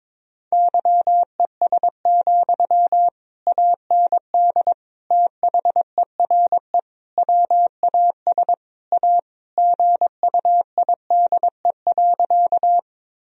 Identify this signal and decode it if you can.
Morse code, heard as a keyed tone, about 22 wpm, 700 Hz. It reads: YES, AND THERE WAS A GUIDE.
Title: Sentences from Top 1000 Words - Review Part 3 - 22wpm